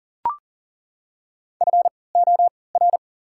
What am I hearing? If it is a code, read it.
Morse code, heard as a keyed tone, about 40 wpm, spaced as for 23 wpm, 700 Hz. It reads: FOR
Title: 3x Character Spacing - Top 200 Words - 40wpm